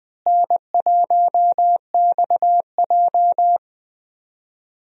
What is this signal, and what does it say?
Morse code, heard as a keyed tone, about 20 wpm, 700 Hz. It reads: N1XJ
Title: USA Call Signs - 4 Characters - Repeated 2x - 20wpm